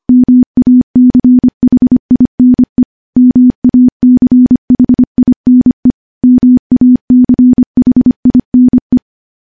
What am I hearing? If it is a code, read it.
Morse code, heard as a keyed tone, about 25 wpm, 250 Hz. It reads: MACHINE MACHINE MACHINE
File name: morse / morse-training-25WPM